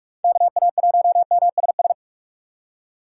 Morse code, heard as a keyed tone, about 43 wpm, 700 Hz. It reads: KA1MSS
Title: USA Call Signs - 6 Characters - 6x Character Spacing - 45wpm